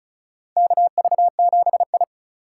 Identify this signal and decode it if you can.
Morse code, heard as a keyed tone, about 34 wpm, 700 Hz. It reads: KV7I
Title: USA Call Signs - 3x Character Spacing - 35wpm